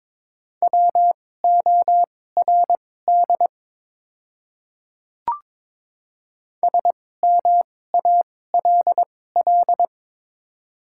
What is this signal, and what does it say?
Morse code, heard as a keyed tone, about 23 wpm, spaced as for 16 wpm, 690 Hz. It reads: WORD SMALL
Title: 2x Character Spacing - Top 300 Words - 22wpm